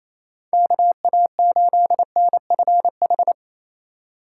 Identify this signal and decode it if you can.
Morse code, heard as a keyed tone, about 28 wpm, 700 Hz. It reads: KA8NFH
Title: USA Call Signs - 6 Characters - 4x Character Spacing - 28wpm